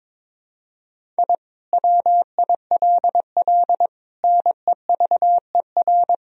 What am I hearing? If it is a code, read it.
Morse code, heard as a keyed tone, about 22 wpm, 700 Hz. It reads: I WILL NEVER
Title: Sentences from Top 1000 Words - Review Part 1 - 22wpm